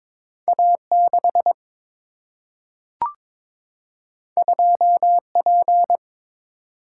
Morse code, heard as a keyed tone, about 22 wpm, 700 Hz. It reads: A 6 2 P
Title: Single-Letter Single-Number 22wpm